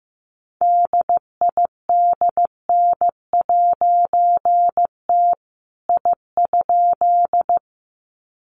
Morse code, heard as a keyed tone, about 15 wpm, 700 Hz. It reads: DIDN'T I?
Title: A Sound Of Thunder 15wpm